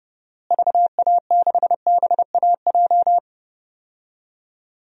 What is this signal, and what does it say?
Morse code, heard as a keyed tone, about 32 wpm, 700 Hz. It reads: VA6BAJ
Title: DXCC-100 Call Signs - 6 Characters - Repeated 3x - 30wpm